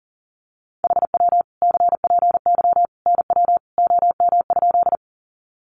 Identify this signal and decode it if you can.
Morse code, heard as a keyed tone, about 40 wpm, 700 Hz. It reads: HW CPY NW OM?
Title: Common QSO Phrases 40wpm